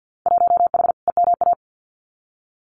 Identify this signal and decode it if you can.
Morse code, heard as a keyed tone, about 50 wpm, 700 Hz. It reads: 1H EMA
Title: Winter Field Day Exchanges 50wpm